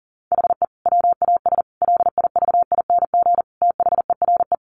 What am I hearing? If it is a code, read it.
Morse code, heard as a keyed tone, about 40 wpm, 700 Hz. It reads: HE WAS LIVING THERE
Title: Sentences from Top 1100 Words 40wpm